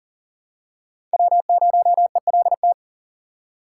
Morse code, heard as a keyed tone, about 37 wpm, 700 Hz. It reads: W0ELT
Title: USA Call Signs - 6x Character Spacing - 40wpm